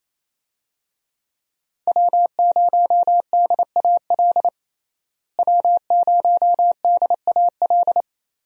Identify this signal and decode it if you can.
Morse code, heard as a keyed tone, about 28 wpm, 700 Hz. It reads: W0DAL W0DAL
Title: USA Call Signs - 5 Characters - Repeated 3x - 28wpm